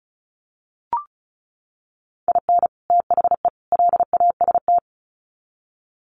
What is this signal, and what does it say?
Morse code, heard as a keyed tone, about 34 wpm, 700 Hz. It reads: IN THE LAST
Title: Sets of 3 Words 35wpm